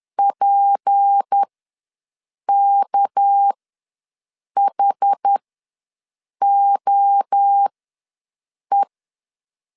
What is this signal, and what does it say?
Morse code, heard as a keyed tone, about 10 wpm, spaced as for 6 wpm, 800 Hz. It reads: PKHOE
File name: LEZIONI / 6 PKX0 / 6 PK ws 10wpm 3745